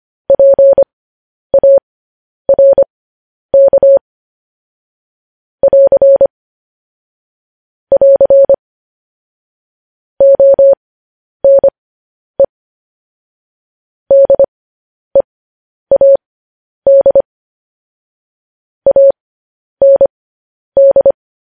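Morse code, heard as a keyed tone, about 25 wpm, spaced as for 10 wpm, 550 Hz. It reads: PARK <AR> <AR> ONE DEAD AND
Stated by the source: Morse code news headlines sent as a reading transcription, preceded by vvv.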